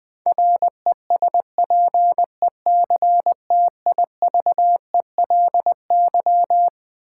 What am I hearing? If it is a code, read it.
Morse code, heard as a keyed tone, about 20 wpm, 700 Hz. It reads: RESPECTIVELY
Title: Build A Word - 4 Letters At A Time - 12-Letter Words - 20wpm